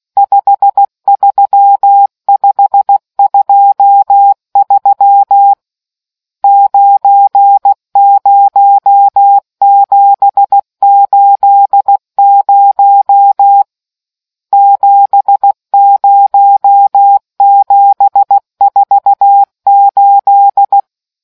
Morse code, heard as a keyed tone, about 16 wpm, 800 Hz. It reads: 53523 90780 70748